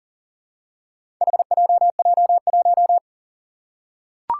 Morse code, heard as a keyed tone, about 40 wpm, 700 Hz. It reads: HJJ1
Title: Four Challenging Letter-Number Combinations 40wpm